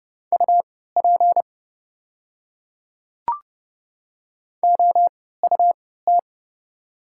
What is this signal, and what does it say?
Morse code, heard as a keyed tone, about 30 wpm, spaced as for 17 wpm, 700 Hz. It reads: UP OUT
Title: 3x Character Spacing - Top 100 Words - 30wpm